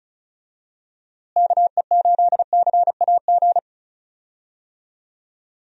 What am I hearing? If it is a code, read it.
Morse code, heard as a keyed tone, about 34 wpm, 700 Hz. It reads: KE8CAG
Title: USA Call Signs - 6 Characters - Repeated 2x - 35wpm